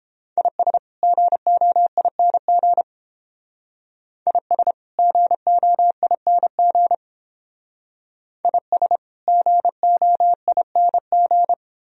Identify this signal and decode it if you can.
Morse code, heard as a keyed tone, about 31 wpm, 700 Hz. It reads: IS GOING IS GOING IS GOING